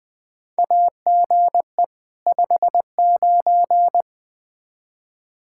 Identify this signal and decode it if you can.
Morse code, heard as a keyed tone, about 20 wpm, 700 Hz. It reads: AGE 59
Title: Practice QSO - Exchange 3 - More Info - 20wpm